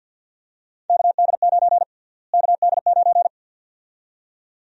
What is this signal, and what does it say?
Morse code, heard as a keyed tone, about 52 wpm, 700 Hz. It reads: KD9 KD9
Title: USA Call Sign Prefixes - Repeated 2x - 50wpm